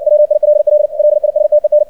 Morse code, heard as a keyed tone, about 60 wpm, 600 Hz. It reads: WILL LISTEN
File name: archived / CRSnet